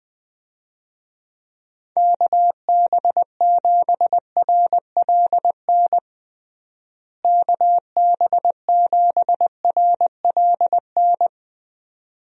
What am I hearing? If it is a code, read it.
Morse code, heard as a keyed tone, about 21 wpm, 700 Hz. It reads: KB7RLN KB7RLN
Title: USA Call Signs - 6 Characters - Repeated 2x - 20wpm